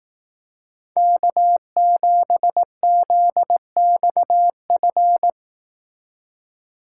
Morse code, heard as a keyed tone, about 18 wpm, 700 Hz. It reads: K7ZXF
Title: Speed Racing - USA Call Signs 15wpm